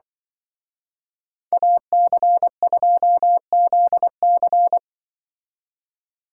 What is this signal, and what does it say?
Morse code, heard as a keyed tone, about 24 wpm, 700 Hz. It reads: AC2ZC
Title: Speed Racing - USA Call Signs 20wpm